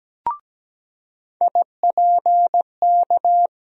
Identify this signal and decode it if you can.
Morse code, heard as a keyed tone, about 17 wpm, 700 Hz. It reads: IPK
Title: USA Call Sign Postfixes - 3 Letters 17wpm